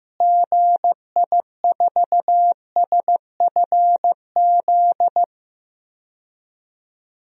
Morse code, heard as a keyed tone, about 15 wpm, 700 Hz. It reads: GI4SFZ
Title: DXCC-100 Call Signs - 6 Characters - Repeated 2x - 15wpm